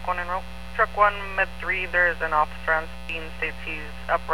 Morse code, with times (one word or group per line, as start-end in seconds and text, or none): none